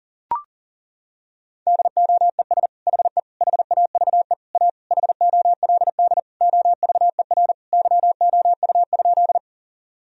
Morse code, heard as a keyed tone, about 40 wpm, 700 Hz. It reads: DOES HE HAVE A HOLD OVER YOU?
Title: Sentences from Top 1000 Words - Review Part 2 - 40wpm